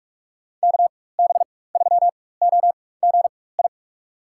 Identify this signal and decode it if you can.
Morse code, heard as a keyed tone, about 42 wpm, spaced as for 21 wpm, 700 Hz. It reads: KB3OGI
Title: USA Call Signs - 4x Character Spacing - 45wpm